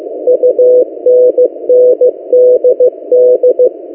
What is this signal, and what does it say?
Morse code, heard as a keyed tone, about 15 wpm, 500 Hz. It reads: UNNDD
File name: M08a- -1355utc-8096khz